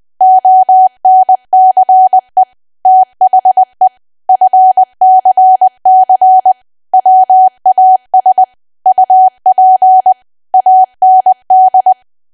Morse code, heard as a keyed tone, about 20 wpm, 750 Hz. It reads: ONCE THE FCC WAS UP AND
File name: cwa / W1AW-Level3 / 20WPM